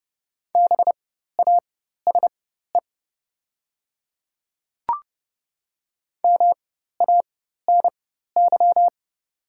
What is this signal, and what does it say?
Morse code, heard as a keyed tone, about 29 wpm, spaced as for 14 wpm, 700 Hz. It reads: BASE MANY
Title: 4x Character Spacing - Top 300 Words - 30wpm